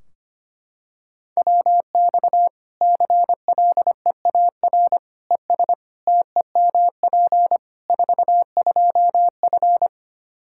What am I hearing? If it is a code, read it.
Morse code, heard as a keyed tone, about 25 wpm, 700 Hz. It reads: WX CLEAR ES TEMP 42F
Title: Practice QSO - Exchange 2 - Info - 25wpm